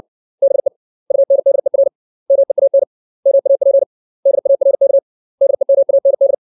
Morse code, heard as a keyed tone, about 60 wpm, 550 Hz. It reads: BE UNDER MEAN MAP DARK DEMAND
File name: 632 words / 632 Words 60WPM